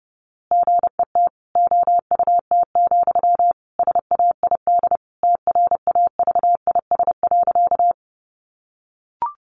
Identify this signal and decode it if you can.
Morse code, heard as a keyed tone, about 30 wpm, 700 Hz. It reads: GET OUT, SAID TRAVIS.
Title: A Sound Of Thunder 30wpm